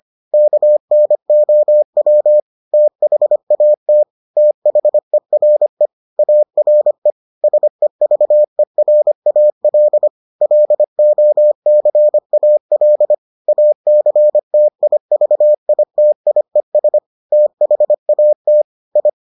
Morse code, heard as a keyed tone, about 25 wpm, 600 Hz. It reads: KNOW THAT THERE ARE SEVERAL LOCAL ACTIVITIES THAT I